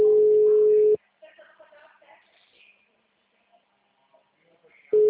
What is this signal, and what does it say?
Ringback tone (Europe, 425 Hz).
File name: out- -2016- - - .114